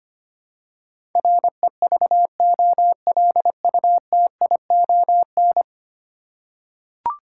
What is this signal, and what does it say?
Morse code, heard as a keyed tone, about 25 wpm, 700 Hz. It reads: REVOLUTION
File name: Ten Letter Words - 25wpm